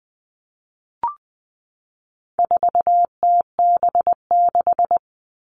Morse code, heard as a keyed tone, about 20 wpm, 700 Hz. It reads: 4TB6
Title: All Cut Numbers - 4-Digit 20wpm